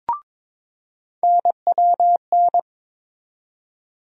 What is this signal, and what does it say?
Morse code, heard as a keyed tone, about 22 wpm, 700 Hz. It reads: NWN
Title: USA Call Sign Postfixes - 3 Letters 22wpm